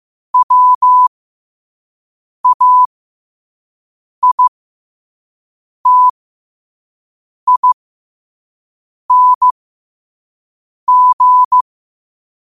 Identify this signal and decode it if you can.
Morse code, heard as a keyed tone, about 15 wpm, spaced as for 5 wpm, 1000 Hz. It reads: WAITING
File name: qotdfiles / 5wpm